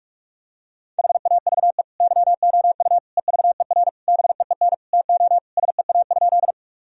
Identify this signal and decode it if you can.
Morse code, heard as a keyed tone, about 46 wpm, 700 Hz. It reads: HAVE YOU EVER BEEN TO SEA?
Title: Sentences from Top 1000 Words - Review Part 1 - 45wpm